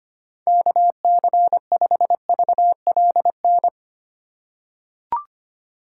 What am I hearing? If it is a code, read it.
Morse code, heard as a keyed tone, about 25 wpm, 700 Hz. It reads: KC5VLN
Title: USA Call Signs - 6 Characters - 8x Character Spacing - 25wpm